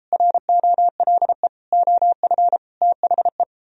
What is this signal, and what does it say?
Morse code, heard as a keyed tone, about 33 wpm, 700 Hz. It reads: ROLE OF THE